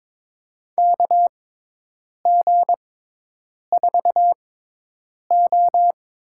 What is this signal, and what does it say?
Morse code, heard as a keyed tone, about 22 wpm, spaced as for 8 wpm, 700 Hz. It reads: KG4O
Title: USA Call Signs - 6 Characters - 6x Character Spacing - 22wpm